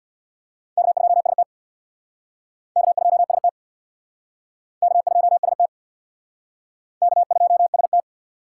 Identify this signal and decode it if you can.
Morse code, heard as a keyed tone, about 63 wpm, 700 Hz. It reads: K2ST K2ST K2ST K2ST